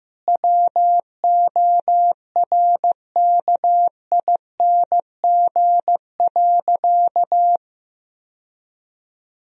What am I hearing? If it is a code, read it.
Morse code, heard as a keyed tone, about 15 wpm, 700 Hz. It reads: WORKING.
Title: Sentences from Top 1000 Words - Review Part 4 - 15wpm